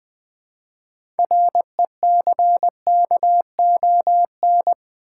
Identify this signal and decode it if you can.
Morse code, heard as a keyed tone, about 20 wpm, 700 Hz. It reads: RECKON
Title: 6 Letter Words - First 3 Letters Given - 20wpm